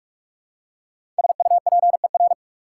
Morse code, heard as a keyed tone, about 45 wpm, 700 Hz. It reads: SUPER
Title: Prefixes and Suffixes 45wpm